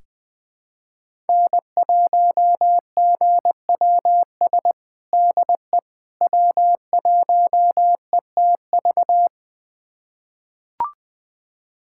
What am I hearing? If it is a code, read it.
Morse code, heard as a keyed tone, about 20 wpm, 700 Hz. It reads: N1GWS DE W1ETV
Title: USA Callsign DE Callsign - 5 Characters - 20wpm